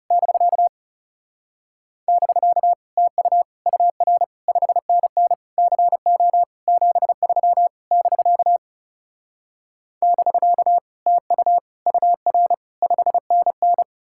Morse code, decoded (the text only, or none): <BK> <BK> TU UR 5NN CO 73 <BK> <BK> TU UR 5NN